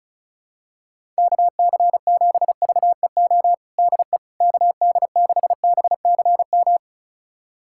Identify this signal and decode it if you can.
Morse code, heard as a keyed tone, about 36 wpm, 700 Hz. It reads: KC7VEO DE KD6BCM
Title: USA Callsign DE Callsign - 6 Characters - 35wpm